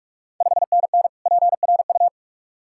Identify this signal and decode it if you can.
Morse code, heard as a keyed tone, about 46 wpm, 700 Hz. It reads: 5NN PRU
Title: Ontario QSO Party Exchange 45wpm